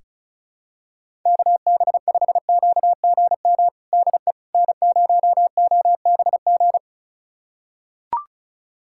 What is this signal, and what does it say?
Morse code, heard as a keyed tone, about 36 wpm, 700 Hz. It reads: KB5QGM DE N0OBG
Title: USA Callsign DE Callsign 35wpm